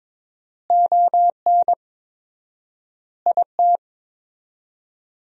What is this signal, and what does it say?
Morse code, heard as a keyed tone, about 22 wpm, 700 Hz. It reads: ON IT
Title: 4x Word-Spacing - Sentences from Top 400 Words - 22wpm